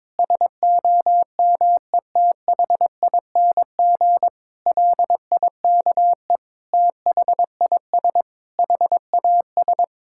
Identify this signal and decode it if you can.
Morse code, heard as a keyed tone, about 22 wpm, 700 Hz. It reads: SOMETHING LIKE THIS HAS
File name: practice / 22wpm / Sentences with 8-Word Phrases - 22wpm